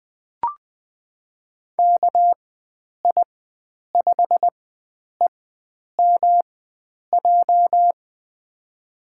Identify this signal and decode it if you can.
Morse code, heard as a keyed tone, about 21 wpm, spaced as for 9 wpm, 700 Hz. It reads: KI5EMJ
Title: USA Call Signs - 6 Characters - 4x Character Spacing - 20wpm